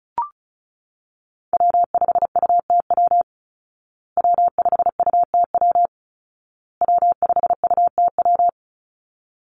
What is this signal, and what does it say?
Morse code, heard as a keyed tone, about 34 wpm, 700 Hz. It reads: W5UTW W5UTW W5UTW
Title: Call Signs Repeated 2x - 35wpm